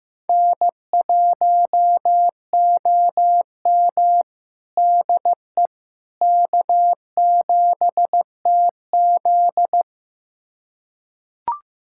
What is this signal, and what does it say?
Morse code, heard as a keyed tone, about 15 wpm, 700 Hz. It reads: N1OM DE K7TZ
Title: USA Callsign DE Callsign - 4 Characters - 15wpm